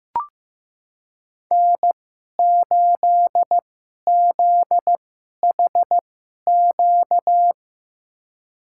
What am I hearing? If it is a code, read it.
Morse code, heard as a keyed tone, about 15 wpm, spaced as for 11 wpm, 700 Hz. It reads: N8ZHQ
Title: USA Call Signs - 5 Characters - 2x Character Spacing - 15wpm